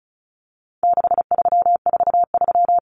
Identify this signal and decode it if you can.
Morse code, heard as a keyed tone, about 34 wpm, 700 Hz. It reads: 6343
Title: Numbers - 4-Digit 35wpm